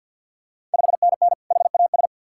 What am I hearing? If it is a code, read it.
Morse code, heard as a keyed tone, about 50 wpm, 690 Hz. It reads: 5NN HAS
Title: Ontario QSO Party Exchange 50wpm